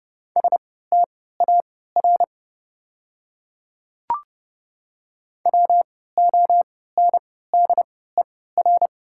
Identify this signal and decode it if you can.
Morse code, heard as a keyed tone, about 30 wpm, spaced as for 17 wpm, 690 Hz. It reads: STAR WONDER
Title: 3x Character Spacing - Top 500 Words - 30wpm